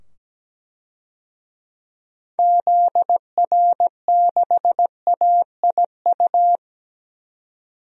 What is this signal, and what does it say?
Morse code, heard as a keyed tone, about 17 wpm, 700 Hz. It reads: ZR6AIU
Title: DXCC-100 Call Signs - 6 Characters - 17wpm